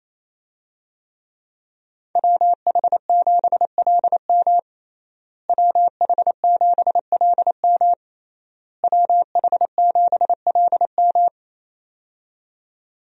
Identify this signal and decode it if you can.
Morse code, heard as a keyed tone, about 28 wpm, 700 Hz. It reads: WH7LM WH7LM WH7LM
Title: USA Call Signs - 5 Characters - Repeated 3x - 28wpm